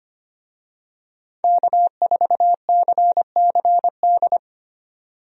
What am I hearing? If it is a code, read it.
Morse code, heard as a keyed tone, about 25 wpm, 700 Hz. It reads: K4CCD
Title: USA Call Signs - 5 Characters - 25wpm